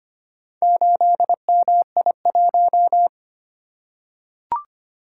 Morse code, heard as a keyed tone, about 25 wpm, 700 Hz. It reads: 8MI1
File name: practice / 25wpm / Four Letter-Number Combinations - 25wpm